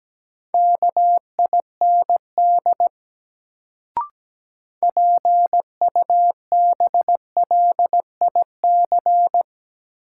Morse code, heard as a keyed tone, about 17 wpm, 700 Hz. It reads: KIND PUBLIC
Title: Sets of 2 Words - New Word Drill - 17wpm